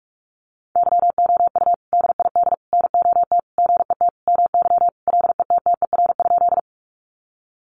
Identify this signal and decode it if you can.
Morse code, heard as a keyed tone, about 44 wpm, 700 Hz. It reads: YOU DID NOT GET MY LETTER?
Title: Sentences from Top 600 Words 45wpm